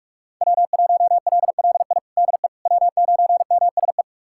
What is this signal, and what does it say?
Morse code, heard as a keyed tone, about 46 wpm, 700 Hz. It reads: W1LLI DE W9MSE
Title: USA Callsign DE Callsign - 5 Characters - 45wpm